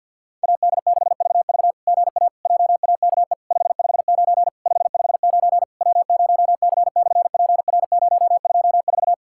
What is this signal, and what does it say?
Morse code, heard as a keyed tone, about 53 wpm, 660 Hz. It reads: AD6VV GA JAKE 559 559 W0C/PR024
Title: SOTA QSO Practice 50wpm